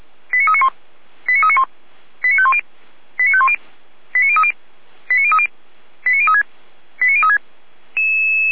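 Five-tone selective call (ZVEI), heard as quick runs of five tones, tones of about 70 ms. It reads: 89391 89391 89510 89510 89030 89030 89036 89036
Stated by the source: ZVEI five-tone selective call received on an SDR.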